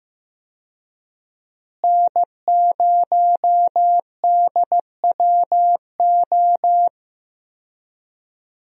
Morse code, heard as a keyed tone, about 15 wpm, 700 Hz. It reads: N0DWO
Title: USA Call Signs - 5 Characters - Repeated 2x - 15wpm